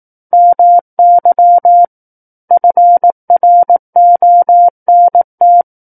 Morse code, heard as a keyed tone, about 18 wpm, 700 Hz. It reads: MY FRONT
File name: Book One / 09-Chapter 9 - 18 WPM